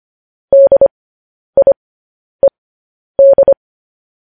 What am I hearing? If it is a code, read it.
Morse code, heard as a keyed tone, about 25 wpm, spaced as for 10 wpm, 550 Hz. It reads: DIED